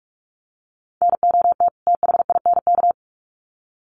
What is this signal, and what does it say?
Morse code, heard as a keyed tone, about 44 wpm, 700 Hz. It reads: NOT THINK
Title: Sets of 2 Words 45wpm